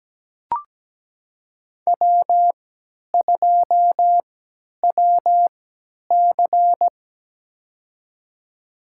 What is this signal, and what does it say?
Morse code, heard as a keyed tone, about 17 wpm, spaced as for 10 wpm, 700 Hz. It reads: W2WC
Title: USA Call Signs - 4 Characters - 3x Character Spacing - 17wpm